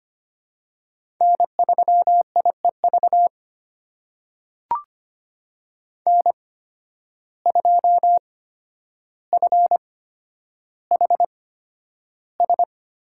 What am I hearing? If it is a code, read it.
Morse code, heard as a keyed tone, about 25 wpm, spaced as for 11 wpm, 700 Hz. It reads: N3IEV N2FHS
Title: USA Call Signs - 5 Characters - 8x Character Spacing - 25wpm